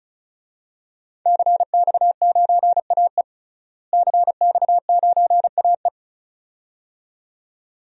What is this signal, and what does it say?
Morse code, heard as a keyed tone, about 34 wpm, 700 Hz. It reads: CX9AE CX9AE
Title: DXCC-100 Call Signs - 5 Characters - Repeated 2x - 35wpm